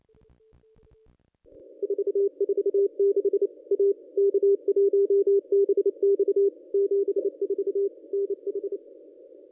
Morse code, heard as a keyed tone, about 29 wpm, 440 Hz. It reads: 446 A K1BX 74 NH
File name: contests / ss-cw-k5tr-op-n5rz -1100